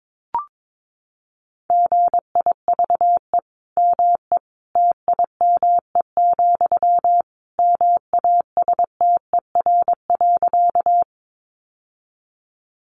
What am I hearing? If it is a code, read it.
Morse code, heard as a keyed tone, about 22 wpm, 700 Hz. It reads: GIVE ME TIME, MASTER.
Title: Sentences from Top 1300 Words 22wpm